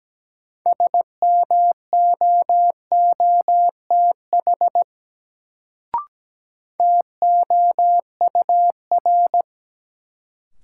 Morse code, heard as a keyed tone, about 17 wpm, 700 Hz. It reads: SMOOTH TOUR
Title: Sentences from Top 1400 Words - New Word Drill Alt - 17wpm